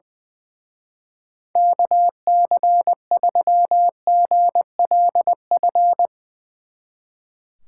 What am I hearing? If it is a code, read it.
Morse code, heard as a keyed tone, about 20 wpm, 700 Hz. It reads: KC3GLF
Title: Speed Racing - USA Call Signs 20wpm